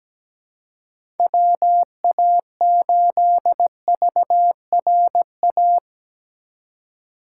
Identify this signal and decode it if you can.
Morse code, heard as a keyed tone, about 17 wpm, 700 Hz. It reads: WA8VRA
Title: USA Call Signs - 6 Characters - 3x Character Spacing - 17wpm